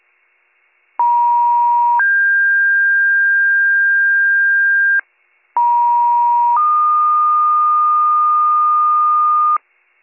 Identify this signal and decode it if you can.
Two-tone page, heard as a one-second tone, then a three-second tone, about 950 then 1600 Hz, about 950 then 1220 Hz, with noise. It is heard from 1.0 to 9.6 s.